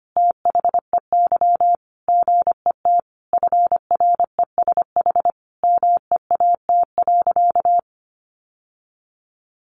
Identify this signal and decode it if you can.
Morse code, heard as a keyed tone, about 25 wpm, 700 Hz. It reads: THEY GET FRESH MEAT.
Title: Sentences from Top 1200 Words 25wpm